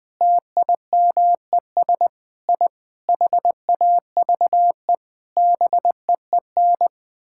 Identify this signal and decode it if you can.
Morse code, heard as a keyed tone, about 20 wpm, 700 Hz. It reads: TIMES I HAVE BEEN
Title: Sentences with 9-Word Phrases - 20wpm